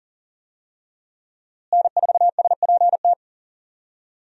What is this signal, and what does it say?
Morse code, heard as a keyed tone, about 40 wpm, 700 Hz. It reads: N4SPT
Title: USA Call Signs - 5 Characters - Repeated 2x - 40wpm